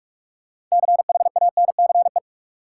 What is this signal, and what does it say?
Morse code, heard as a keyed tone, about 42 wpm, 700 Hz. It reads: CHANCE
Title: Sets of 3 Words - New Word Drill - 45wpm